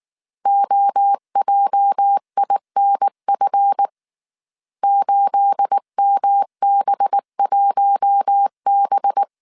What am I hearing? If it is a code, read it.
Morse code, heard as a keyed tone, about 19 wpm, 800 Hz. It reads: OJINF 8MB1B
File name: CW / DET / 1 9 1500C S A 18wpm 2459